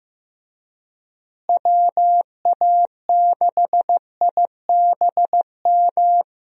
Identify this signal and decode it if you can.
Morse code, heard as a keyed tone, about 15 wpm, 700 Hz. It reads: WA6IBM
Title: USA Call Signs - 6 Characters - Repeated 2x - 15wpm